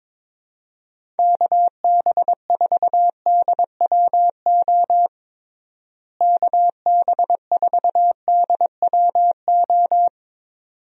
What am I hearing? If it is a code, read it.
Morse code, heard as a keyed tone, about 22 wpm, 700 Hz. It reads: KB4DWO KB4DWO
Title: USA Call Signs - 6 Characters - Repeated 2x - 22wpm